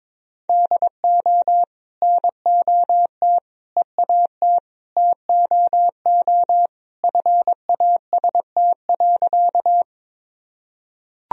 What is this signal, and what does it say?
Morse code, heard as a keyed tone, about 22 wpm, 700 Hz. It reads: DO NOT EAT TOO FAST.
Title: Sentences from Top 1000 Words - Review Part 2 - 22wpm